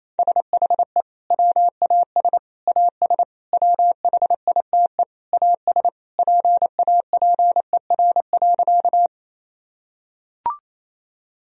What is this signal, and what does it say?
Morse code, heard as a keyed tone, about 28 wpm, 700 Hz. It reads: SHE WAS AS WHITE AS PAPER.
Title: Sentences from Top 1000 Words - Review Part 1 - 28wpm